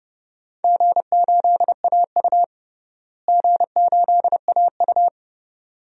Morse code, heard as a keyed tone, about 32 wpm, 700 Hz. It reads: G8AU G8AU
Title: DXCC-100 Call Signs - 4 Characters - Repeated 2x - 30wpm